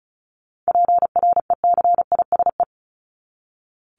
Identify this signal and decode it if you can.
Morse code, heard as a keyed tone, about 34 wpm, 700 Hz. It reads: PRECISE
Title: LY Suffix - Top 100 Words with Base Words 35wpm